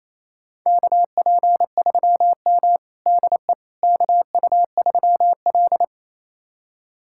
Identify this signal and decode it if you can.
Morse code, heard as a keyed tone, about 28 wpm, 700 Hz. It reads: KP3M DE KU3L
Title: USA Callsign DE Callsign - 4 Characters - 28wpm